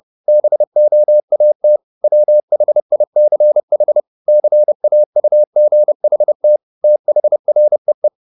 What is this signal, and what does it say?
Morse code, heard as a keyed tone, about 30 wpm, 600 Hz. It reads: BOAT WHICH CAUGHT THREE
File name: the old man and the sea / the old man and the sea 1 30wpm